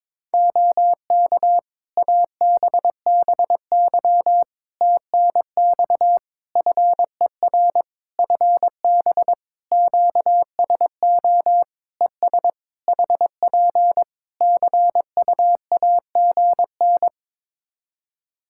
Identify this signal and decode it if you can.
Morse code, heard as a keyed tone, about 22 wpm, 700 Hz. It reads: OK ABBY TNX FER FB QSO ES HP CUAGN
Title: Practice QSO - Exchange 4 - Ending - 22wpm